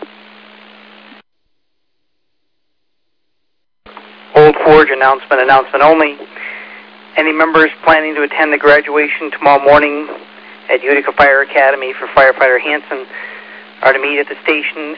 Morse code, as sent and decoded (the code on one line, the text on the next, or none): none